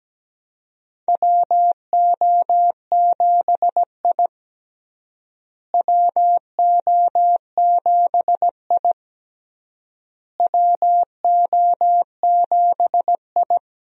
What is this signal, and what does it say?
Morse code, heard as a keyed tone, about 17 wpm, 700 Hz. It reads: WO7I WO7I WO7I
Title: USA Call Signs - 4 Characters - Repeated 3x - 17wpm